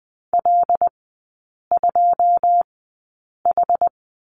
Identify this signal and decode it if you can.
Morse code, heard as a keyed tone, about 20 wpm, spaced as for 8 wpm, 700 Hz. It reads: L2H
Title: 107 Lesson 18e - Letter m - Three Characters 20wpm